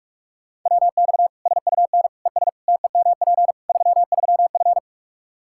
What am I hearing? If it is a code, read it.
Morse code, heard as a keyed tone, about 42 wpm, 700 Hz. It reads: WX SUN ES TEMP 33F
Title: K1USN SST QSO Practice 45wpm